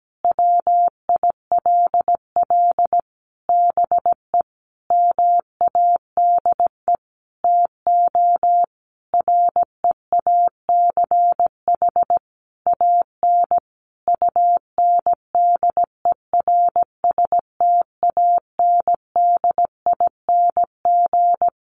Morse code, heard as a keyed tone, about 17 wpm, 700 Hz. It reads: WILL BE MADE TO REACH AN UNDERSTANDING
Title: Sentences from Top 1500 Words 17wpm